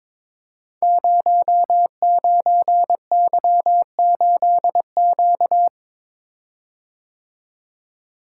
Morse code, heard as a keyed tone, about 23 wpm, 700 Hz. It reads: 09Y8Q
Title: Code Groups - 5 Letter:Numbers - 22wpm